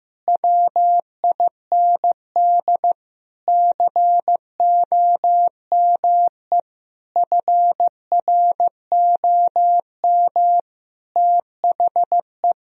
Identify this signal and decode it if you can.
Morse code, heard as a keyed tone, about 15 wpm, 700 Hz. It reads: WIND COME FROM THE
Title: Sentences from Top 1000 Words - Review Part 2 - 15wpm